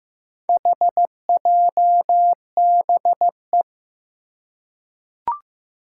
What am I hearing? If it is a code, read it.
Morse code, heard as a keyed tone, about 15 wpm, 700 Hz. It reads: HJBE
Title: Four Challenging Letter-Number Combinations 15wpm